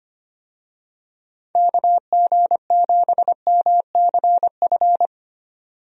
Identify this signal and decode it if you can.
Morse code, heard as a keyed tone, about 25 wpm, 700 Hz. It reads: KG7MCF
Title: USA Call Signs - 6 Characters - 25wpm